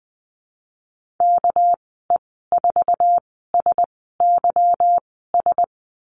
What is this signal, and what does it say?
Morse code, heard as a keyed tone, about 20 wpm, spaced as for 14 wpm, 700 Hz. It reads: KE4SYS
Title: 231 Lesson 37g - Letter x - Call Signs 20wpm